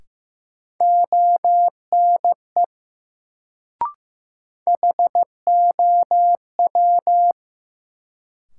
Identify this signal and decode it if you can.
Morse code, heard as a keyed tone, about 15 wpm, 700 Hz. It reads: ONE HOW
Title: Sets of 2 Words - New Word Drill - 15wpm